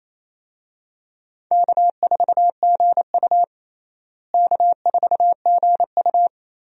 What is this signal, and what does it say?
Morse code, heard as a keyed tone, about 28 wpm, 700 Hz. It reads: K4GU K4GU
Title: USA Call Signs - 4 Characters - Repeated 3x - 28wpm